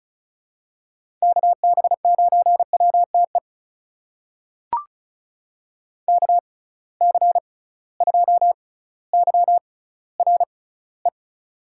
Morse code, heard as a keyed tone, about 36 wpm, spaced as for 18 wpm, 700 Hz. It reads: KB9WTE KC2YRE
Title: USA Call Signs - 6 Characters - 6x Character Spacing - 35wpm